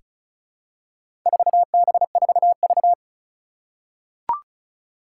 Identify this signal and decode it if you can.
Morse code, heard as a keyed tone, about 36 wpm, 700 Hz. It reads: 4B4V
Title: Four Challenging Letter-Number Combinations 35wpm